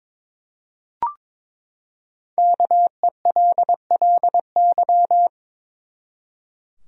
Morse code, heard as a keyed tone, about 22 wpm, 700 Hz. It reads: KELLY
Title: USA Common Names - Max 5 Letters - 22wpm